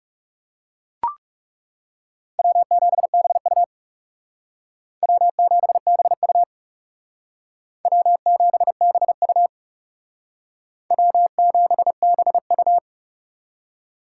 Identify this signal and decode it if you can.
Morse code, heard as a keyed tone, about 38 wpm, 700 Hz. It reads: W7BU W7BU W7BU W7BU
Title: Speed Racing - USA 1x2 Call Signs 30wpm